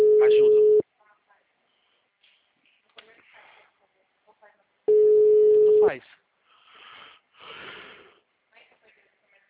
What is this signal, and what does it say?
Ringback tone (Europe, 425 Hz), with speech.